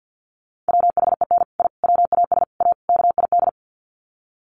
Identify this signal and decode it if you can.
Morse code, heard as a keyed tone, about 50 wpm, 700 Hz. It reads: WHEN I WAS A KID
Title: Sets of 5 Words 50wpm